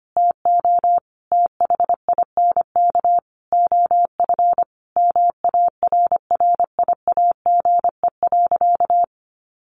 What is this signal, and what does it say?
Morse code, heard as a keyed tone, about 25 wpm, 700 Hz. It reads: TO THINK OF MARRIAGE.
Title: Sentences from Top 800 Words 25wpm